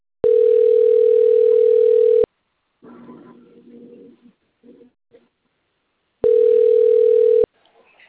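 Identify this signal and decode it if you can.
Ringback tone (North America), 440 + 480 Hz.
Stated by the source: Outgoing telephone call from Limeira.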